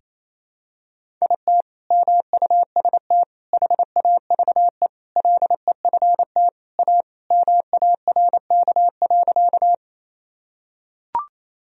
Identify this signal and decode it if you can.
Morse code, heard as a keyed tone, about 28 wpm, 700 Hz. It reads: IT MUST HAVE LEFT A MARK.
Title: Sentences from Top 1000 Words - Review Part 1 - 28wpm